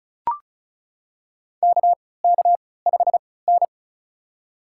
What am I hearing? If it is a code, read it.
Morse code, heard as a keyed tone, about 34 wpm, spaced as for 20 wpm, 690 Hz. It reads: KK5N
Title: USA Call Signs - 3x Character Spacing - 35wpm